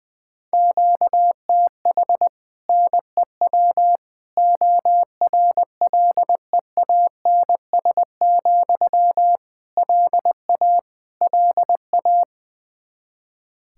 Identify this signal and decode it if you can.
Morse code, heard as a keyed tone, about 20 wpm, 700 Hz. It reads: QTH NEW ORLEANS, LA LA
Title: Practice QSO - Exchange 1 - Beginning - 20wpm